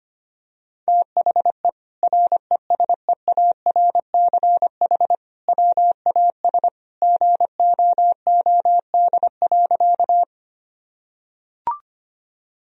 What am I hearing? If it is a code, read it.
Morse code, heard as a keyed tone, about 25 wpm, 700 Hz. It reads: THE RESEARCH WAS GOOD.
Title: Sentences from Top 1000 Words - Review Part 4 - 25wpm